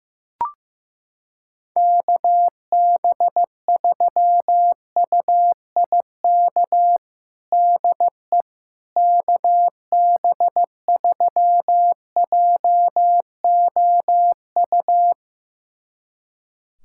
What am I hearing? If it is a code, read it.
Morse code, heard as a keyed tone, about 15 wpm, 700 Hz. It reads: KB3UIK DE KB3JOU
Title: USA Callsign DE Callsign - 6 Characters - 15wpm